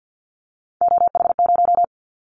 Morse code, heard as a keyed tone, about 50 wpm, 700 Hz. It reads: OH0
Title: Callsign Prefixes for Top 100 DXCC - 50wpm